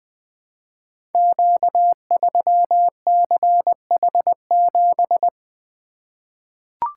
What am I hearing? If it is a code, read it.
Morse code, heard as a keyed tone, about 20 wpm, 700 Hz. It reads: Q3CH7
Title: Code Groups - 5 Letter:Numbers - 20wpm